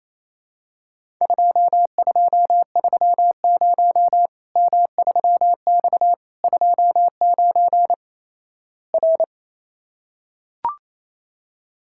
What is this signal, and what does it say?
Morse code, heard as a keyed tone, about 28 wpm, 700 Hz. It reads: 2230 M3X 29 R
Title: Worked All Europe DX Contest - QTC Exchange - 28wpm